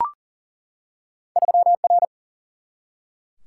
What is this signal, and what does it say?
Morse code, heard as a keyed tone, about 41 wpm, 700 Hz. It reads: 3R
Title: Single-Letter Single-Number 40wpm